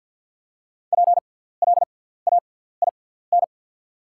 Morse code, heard as a keyed tone, about 48 wpm, spaced as for 17 wpm, 690 Hz. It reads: PLAIN
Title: 6x Character Spacing - Top 500 Words - 50wpm